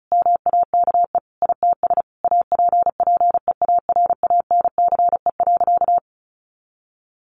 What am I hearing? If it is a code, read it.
Morse code, heard as a keyed tone, about 34 wpm, 700 Hz. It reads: MAKE ITS APPEARANCE.
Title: Sentences from Top 1100 Words 35wpm